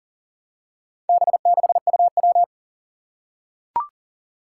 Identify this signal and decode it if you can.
Morse code, heard as a keyed tone, about 40 wpm, 700 Hz. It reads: B6UW
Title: Four Challenging Letter-Number Combinations 40wpm